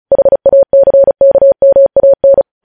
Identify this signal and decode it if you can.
Morse code, heard as a keyed tone, about 36 wpm, 550 Hz. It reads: HACKMAN
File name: podcast / News headlines-1x-35x35